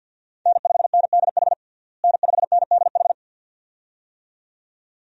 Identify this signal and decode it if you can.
Morse code, heard as a keyed tone, about 51 wpm, 700 Hz. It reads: N5NDH N5NDH
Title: USA Call Signs - 5 Characters - Repeated 2x - 50wpm